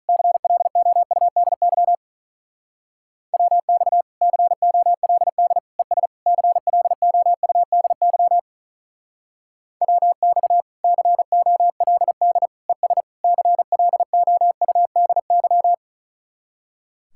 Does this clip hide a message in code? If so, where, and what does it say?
Yes: Morse code from 0.1 to 15.7 s, heard as a keyed tone, about 45 wpm, 700 Hz. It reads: CLOUDY WX COLD ES CLOUDY WX COLD ES CLOUDY